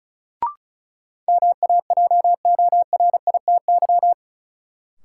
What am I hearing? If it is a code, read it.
Morse code, heard as a keyed tone, about 36 wpm, 700 Hz. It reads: MAJORITY
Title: Sets of 3 Words - New Word Drill - 35wpm